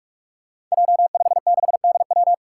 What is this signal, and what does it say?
Morse code, heard as a keyed tone, about 44 wpm, 700 Hz. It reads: J56DW
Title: Five Challenging Letter-Number Combinations 45wpm